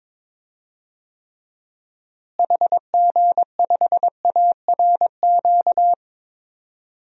Morse code, heard as a keyed tone, about 22 wpm, 700 Hz. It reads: HG5ARQ
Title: DXCC-100 Call Signs - 6 Characters - Repeated 2x - 22wpm